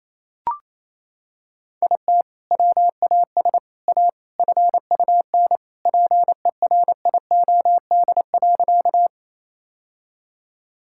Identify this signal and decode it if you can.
Morse code, heard as a keyed tone, about 28 wpm, 700 Hz. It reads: IT WAS A FUN PERIOD.
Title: Sentences from Top 1000 Words - Review Part 5 - 28wpm